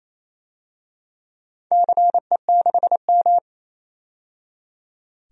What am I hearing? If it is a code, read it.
Morse code, heard as a keyed tone, about 28 wpm, 700 Hz. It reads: CE6M
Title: DXCC-100 Call Signs - 4 Characters - 28wpm